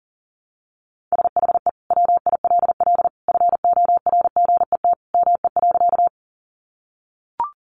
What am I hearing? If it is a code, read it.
Morse code, heard as a keyed tone, about 40 wpm, 700 Hz. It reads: SHE WILL FORGET ME.
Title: Sentences from Top 1200 Words 40wpm